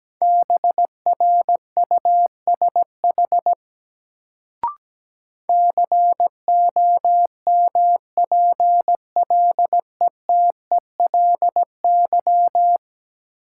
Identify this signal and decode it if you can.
Morse code, heard as a keyed tone, about 17 wpm, 700 Hz. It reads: BRUSH COMPLETELY
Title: Sentences from Top 1100 Words - New Word Drill - 17wpm